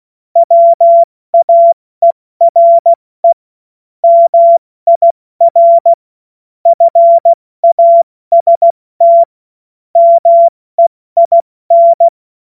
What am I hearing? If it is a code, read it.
Morse code, heard as a keyed tone, about 16 wpm, 680 Hz. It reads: WAERE MIR FAST MEIN
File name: Morsekurs / Witze / Witz 27. Juli 2021, 70 F80